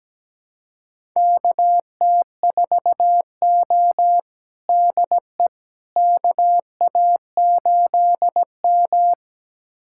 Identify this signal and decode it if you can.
Morse code, heard as a keyed tone, about 17 wpm, 700 Hz. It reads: KT4O DE KA8M
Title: USA Callsign DE Callsign - 4 Characters - 17wpm